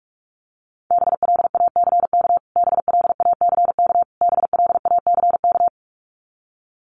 Morse code, heard as a keyed tone, about 44 wpm, 700 Hz. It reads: BLACK BLACK BLACK
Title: Top 500 Words Repeated 45wpm